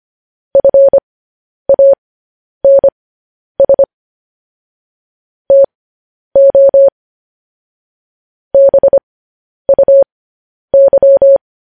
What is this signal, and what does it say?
Morse code, heard as a keyed tone, about 25 wpm, spaced as for 10 wpm, 550 Hz. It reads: FANS TO BUY
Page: news-headlines-in-morse-code-at-10-wpm